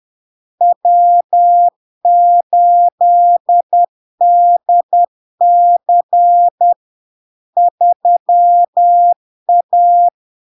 Morse code, heard as a keyed tone, about 10 wpm, 700 Hz. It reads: W8DC 3A
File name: LARK-FD-2022-Log-in-CW-10wpm-700-Hz